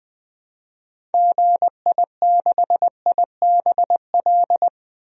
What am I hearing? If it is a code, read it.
Morse code, heard as a keyed tone, about 20 wpm, 700 Hz. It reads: GI6IBL
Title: DXCC-100 Call Signs - 6 Characters - 20wpm